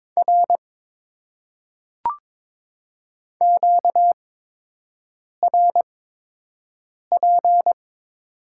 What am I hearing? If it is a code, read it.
Morse code, heard as a keyed tone, about 22 wpm, spaced as for 6 wpm, 700 Hz. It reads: R QRP
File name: Top 90 QSO Elements - 8x Character Spacing - 22wpm